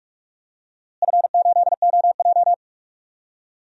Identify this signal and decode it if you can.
Morse code, heard as a keyed tone, about 44 wpm, 700 Hz. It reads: F8OJ
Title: DXCC-100 Call Signs - 4 Characters - Repeated 3x - 45wpm